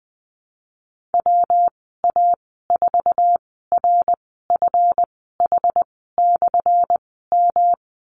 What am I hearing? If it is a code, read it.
Morse code, heard as a keyed tone, about 19 wpm, spaced as for 14 wpm, 700 Hz. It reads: WA4RFH/M
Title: 179 Lesson 29g - Slash - Call Signs 20wpm